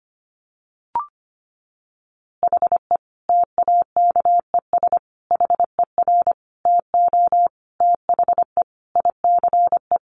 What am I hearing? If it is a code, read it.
Morse code, heard as a keyed tone, about 25 wpm, 700 Hz. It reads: HE TAKES HER TO THE ICE
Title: Easy Phrases 25wpm